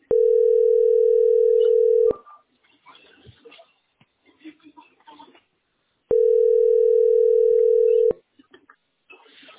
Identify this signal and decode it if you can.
Ringback tone (North America), 440 + 480 Hz.